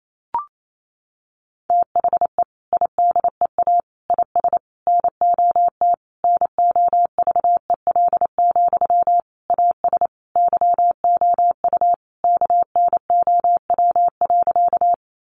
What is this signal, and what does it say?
Morse code, heard as a keyed tone, about 28 wpm, 700 Hz. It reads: THE IDEA IS NOT NOVEL, AS YOU KNOW.
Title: Sentences from Top 1300 Words 28wpm